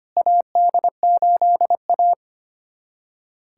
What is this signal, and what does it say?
Morse code, heard as a keyed tone, about 25 wpm, 700 Hz. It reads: AD8A
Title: USA Callsign DE Callsign - 4 Characters - 25wpm